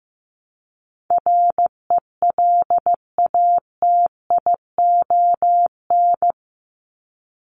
Morse code, heard as a keyed tone, about 15 wpm, 700 Hz. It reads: RELATION
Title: ES:S Suffix - Top 100 Words with Base Words 15wpm